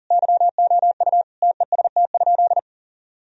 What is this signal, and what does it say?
Morse code, heard as a keyed tone, about 40 wpm, 700 Hz. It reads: YOU TEST?
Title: Sentences from Top 1000 Words - Review Part 3 - 40wpm